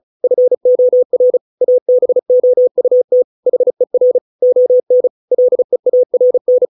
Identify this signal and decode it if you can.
Morse code, heard as a keyed tone, about 34 wpm, 500 Hz. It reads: FOR ABOUT HER ON LEARN